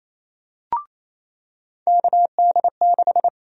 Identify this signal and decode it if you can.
Morse code, heard as a keyed tone, about 28 wpm, 700 Hz. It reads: KD6
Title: USA Call Sign Prefixes - 28wpm